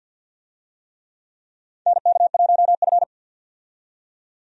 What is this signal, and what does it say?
Morse code, heard as a keyed tone, about 47 wpm, 700 Hz. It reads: NK1F
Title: USA Call Signs - 4 Characters - 50wpm